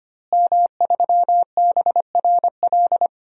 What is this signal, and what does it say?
Morse code, heard as a keyed tone, about 25 wpm, 700 Hz. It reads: M3BRL
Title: DXCC-100 Call Signs - 5 Characters - Repeated 3x - 25wpm